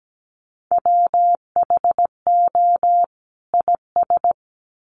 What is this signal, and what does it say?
Morse code, heard as a keyed tone, about 17 wpm, 700 Hz. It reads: WHO IS
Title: Sets of 2 Words 17wpm